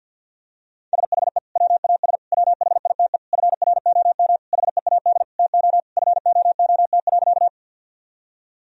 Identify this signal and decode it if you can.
Morse code, heard as a keyed tone, about 50 wpm, 700 Hz. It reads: SHE WAS WHITE FROM HEAD TO FOOT.
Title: Sentences from Top 1000 Words - Review Part 3 - 50wpm